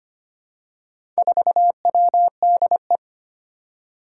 Morse code, heard as a keyed tone, about 25 wpm, 700 Hz. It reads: 4WDE
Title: Four Challenging Letter-Number Combinations 25wpm